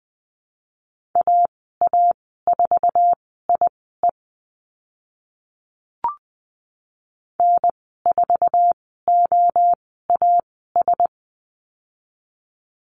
Morse code, heard as a keyed tone, about 20 wpm, spaced as for 14 wpm, 700 Hz. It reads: AA4IE N4OAS
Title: 048 Lesson 9g - Number 4 - Call Signs 20wpm